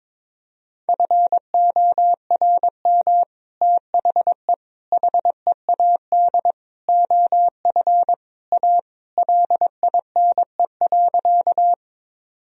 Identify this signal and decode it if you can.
Morse code, heard as a keyed tone, about 22 wpm, 700 Hz. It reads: FORM THE HEAD OF A LINE.
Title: Sentences from Top 1000 Words - Review Part 1 - 22wpm